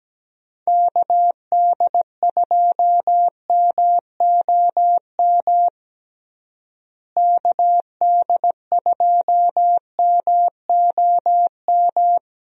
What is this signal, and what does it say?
Morse code, heard as a keyed tone, about 17 wpm, 700 Hz. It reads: KD2MOM KD2MOM
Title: USA Call Signs - 6 Characters - Repeated 3x - 17wpm